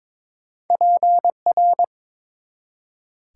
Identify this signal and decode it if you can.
Morse code, heard as a keyed tone, about 21 wpm, 700 Hz. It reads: PR
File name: practice / 22wpm / ARRL Sections v2 - 22wpm